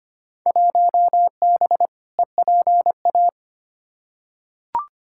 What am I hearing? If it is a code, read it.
Morse code, heard as a keyed tone, about 25 wpm, 700 Hz. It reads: 1B EPA
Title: Field Day Exchanges Encore - 25wpm